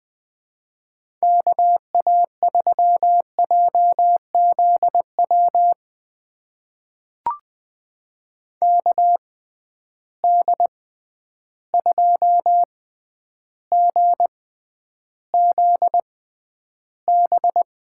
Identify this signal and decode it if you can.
Morse code, heard as a keyed tone, about 21 wpm, spaced as for 10 wpm, 700 Hz. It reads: KA3JZW KD2GZB
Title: USA Call Signs - 6 Characters - 6x Character Spacing - 20wpm